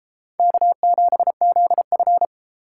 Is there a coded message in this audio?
Yes: Morse code from 0.4 to 2.3 s, heard as a keyed tone, about 32 wpm, 700 Hz. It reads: K7ZF